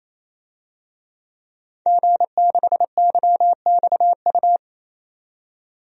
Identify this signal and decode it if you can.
Morse code, heard as a keyed tone, about 27 wpm, 700 Hz. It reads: G6YXU
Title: DXCC-100 Call Signs - 5 Characters - 28wpm